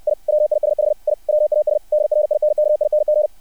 Morse code, heard as a keyed tone, about 60 wpm, 600 Hz. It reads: I WISH I WAS DRINKING
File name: archived / CRSnet